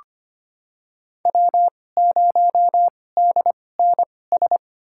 Morse code, heard as a keyed tone, about 25 wpm, spaced as for 18 wpm, 700 Hz. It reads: W0DNS
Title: USA Call Signs - 5 Characters - 2x Character Spacing - 25wpm